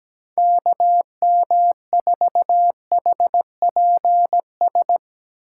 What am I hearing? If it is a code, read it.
Morse code, heard as a keyed tone, about 18 wpm, 700 Hz. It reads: KM4HPS
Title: Speed Racing - USA 2x3 Call Signs 17wpm